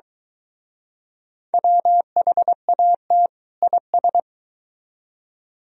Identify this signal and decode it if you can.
Morse code, heard as a keyed tone, about 23 wpm, 700 Hz. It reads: WHAT IS